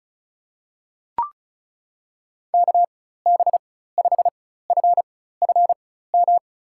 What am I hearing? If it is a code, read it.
Morse code, heard as a keyed tone, about 34 wpm, spaced as for 16 wpm, 700 Hz. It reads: KB5FFM
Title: USA Call Signs - 6 Characters - 4x Character Spacing - 35wpm